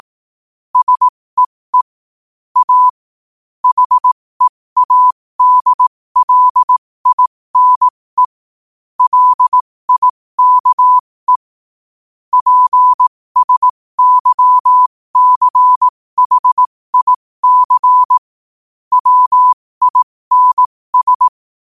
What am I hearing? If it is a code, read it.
Morse code, heard as a keyed tone, about 18 wpm, spaced as for 15 wpm, 1000 Hz. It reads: SEE A HEADLINE LIKE PSYCHIC WINS
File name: qotdfiles / 15wpm